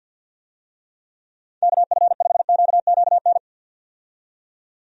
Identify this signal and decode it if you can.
Morse code, heard as a keyed tone, about 51 wpm, 700 Hz. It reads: KF5QQN
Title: USA Call Signs - 6 Characters - 50wpm